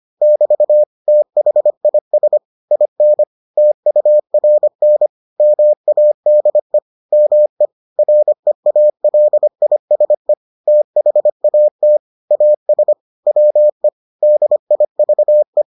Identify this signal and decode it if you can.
Morse code, heard as a keyed tone, about 25 wpm, 600 Hz. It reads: <BT> THIS IN TURN MADE ME REALISE THAT AS WE DIVE